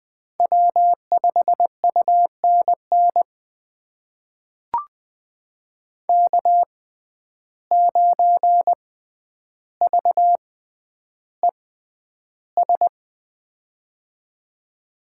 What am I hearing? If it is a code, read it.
Morse code, heard as a keyed tone, about 20 wpm, spaced as for 10 wpm, 700 Hz. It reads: W5UNN K9VES
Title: USA Call Signs - 5 Characters - 6x Character Spacing - 20wpm